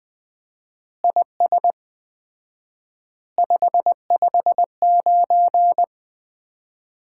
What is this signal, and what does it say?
Morse code, heard as a keyed tone, about 21 wpm, 700 Hz. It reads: IS 559
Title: Common QSO Phrases - 4x Word-Spacing - 20wpm